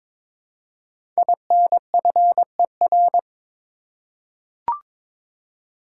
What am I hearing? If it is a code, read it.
Morse code, heard as a keyed tone, about 22 wpm, 700 Hz. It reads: INFER
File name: 5 Letter Words - First 3 Letters Given - 22wpm